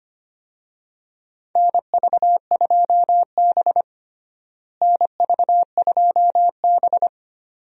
Morse code, heard as a keyed tone, about 25 wpm, 700 Hz. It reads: NV2B NV2B
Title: USA Call Signs - 4 Characters - Repeated 2x - 25wpm